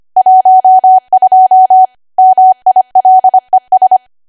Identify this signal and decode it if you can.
Morse code, heard as a keyed tone, about 25 wpm, 750 Hz. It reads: 12 MILES
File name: Archive / 25 WPM / 25WPM